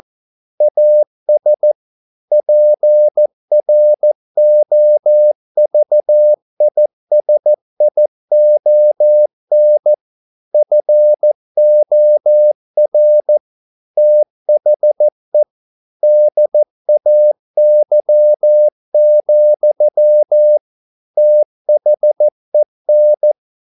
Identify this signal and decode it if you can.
Morse code, heard as a keyed tone, about 14 wpm, 600 Hz. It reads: AS PROVISION FOR THE DAY, THEN
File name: mp3 files / a dogs mistake - 14wpm